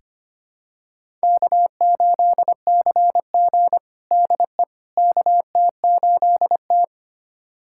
Morse code, heard as a keyed tone, about 25 wpm, 700 Hz. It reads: K8CG DE KT8T
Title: USA Callsign DE Callsign - 4 Characters - 25wpm